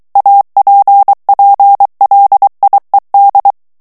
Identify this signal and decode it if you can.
Morse code, heard as a keyed tone, about 23 wpm, 800 Hz. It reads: APPLIED
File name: ANGLAIS texte clair 3 vitesse 22 mots minute